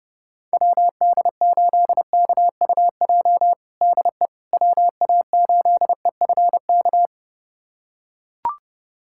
Morse code, heard as a keyed tone, about 30 wpm, 700 Hz. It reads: WD8KUJ DE WA8EFK
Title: USA Callsign DE Callsign 30wpm